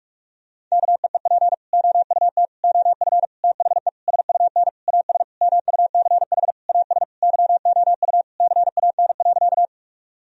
Morse code, heard as a keyed tone, about 41 wpm, 700 Hz. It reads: KEEP OUT OF THE SUN AS MUCH AS YOU CAN.
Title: Sentences from Top 1000 Words - Review Part 1 - 45wpm